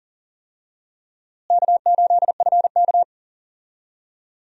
Morse code, heard as a keyed tone, about 37 wpm, 700 Hz. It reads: K8FK
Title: USA Call Signs - 4 Characters - Repeated 2x - 40wpm